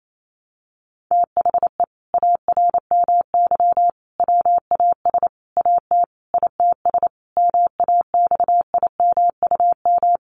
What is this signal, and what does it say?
Morse code, heard as a keyed tone, about 28 wpm, 700 Hz. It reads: THE ARMY WAS AT ITS MAXIMUM
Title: Sentences from Top 1300 Words 28wpm